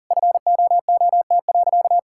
Morse code, heard as a keyed tone, about 40 wpm, 700 Hz. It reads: FOOT.
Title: Sentences from Top 1000 Words - Review Part 6 - 40wpm